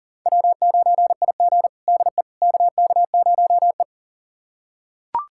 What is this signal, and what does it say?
Morse code, heard as a keyed tone, about 40 wpm, 700 Hz. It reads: W9IG DE KK0E
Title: USA Callsign DE Callsign - 4 Characters - 40wpm